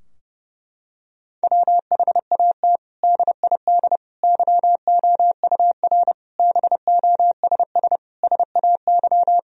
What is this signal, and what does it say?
Morse code, heard as a keyed tone, about 32 wpm, 700 Hz. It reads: WHAT DID YOUR BOSS SAY
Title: Sentences from Top 1000 Words - Review Part 4 - 30wpm